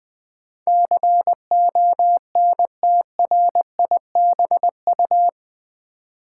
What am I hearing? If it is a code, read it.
Morse code, heard as a keyed tone, about 20 wpm, 700 Hz. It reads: CONTRIBU
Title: Build A Word - 4 Letters At A Time - 12-Letter Words - 20wpm